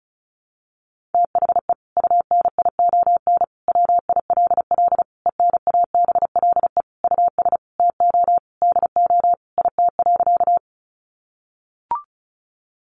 Morse code, heard as a keyed tone, about 34 wpm, 700 Hz. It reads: THE UNION WILL ENABLE US TO DO IT.
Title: Sentences from Top 1500 Words 35wpm